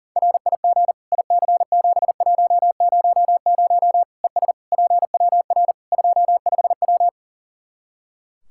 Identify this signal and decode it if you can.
Morse code, heard as a keyed tone, about 40 wpm, 700 Hz. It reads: RIG IC7100 ES PWR 25W
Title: K1USN SST QSO Practice 40wpm